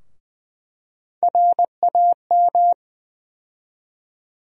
Morse code, heard as a keyed tone, about 20 wpm, 700 Hz. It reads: RAM
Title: USA Call Sign Postfixes - 3 Letters 20wpm